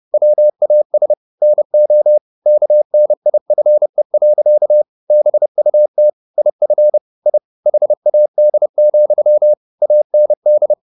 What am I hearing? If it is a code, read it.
Morse code, heard as a keyed tone, about 30 wpm, 600 Hz. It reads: WAS NO KNIFE. BUT IF I HAD, AND